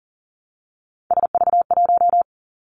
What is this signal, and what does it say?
Morse code, heard as a keyed tone, about 40 wpm, 700 Hz. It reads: SV1
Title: Callsign Prefixes for Top 50 DXCC - 40wpm